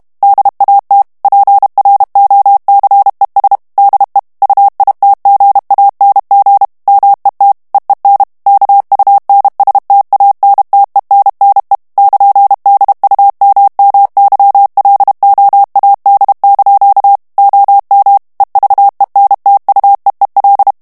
Morse code, heard as a keyed tone, about 32 wpm, 800 Hz. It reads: DAT PROCES DE UITGANG MET EEN KUNSTANTENNE <KN>DUMMYLOAD) OM EVENTUEEL
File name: HOLLANDAIS texte clair 2 vitesse 30 mots minute